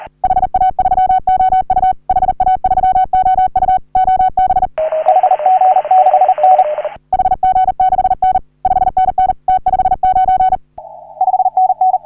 Morse code, heard as a keyed tone, about 40 wpm, 750 Hz. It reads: HA3OU HA3OU OB HG6N HG6N 5NN T59 5NN
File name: CW / HA3OU RHOBCW REC